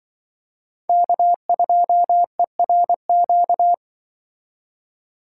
Morse code, heard as a keyed tone, about 24 wpm, 700 Hz. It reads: K2ERQ
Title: Speed Racing - USA Call Signs 20wpm